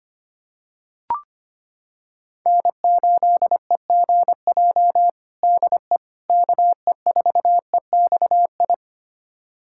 Morse code, heard as a keyed tone, about 25 wpm, 700 Hz. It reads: N8EGJ DE KE4EXI
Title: USA Callsign DE Callsign 25wpm